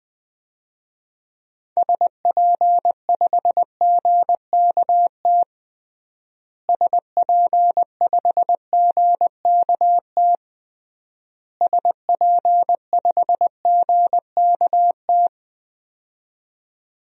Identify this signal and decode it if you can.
Morse code, heard as a keyed tone, about 20 wpm, 700 Hz. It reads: SP5GKT SP5GKT SP5GKT
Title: DXCC-100 Call Signs - 6 Characters - Repeated 3x - 20wpm